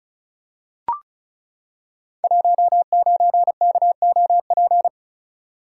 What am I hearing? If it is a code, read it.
Morse code, heard as a keyed tone, about 36 wpm, 700 Hz. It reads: 19KOP
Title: Code Groups - 5 Letter:Numbers - 35wpm